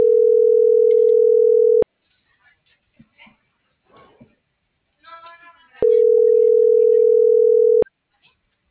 Ringback tone (North America), 440 + 480 Hz.